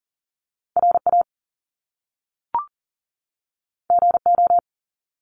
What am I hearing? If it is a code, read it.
Morse code, heard as a keyed tone, about 39 wpm, 700 Hz. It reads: RA GO
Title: Top Two Letter Combinations in Words 40wpm